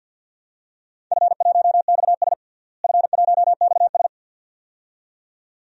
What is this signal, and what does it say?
Morse code, heard as a keyed tone, about 47 wpm, 700 Hz. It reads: F1XS F1XS
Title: DXCC-100 Call Signs - 4 Characters - Repeated 2x - 50wpm